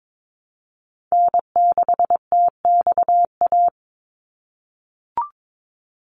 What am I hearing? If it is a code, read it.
Morse code, heard as a keyed tone, about 22 wpm, 700 Hz. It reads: N6TXA
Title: USA Call Signs 22wpm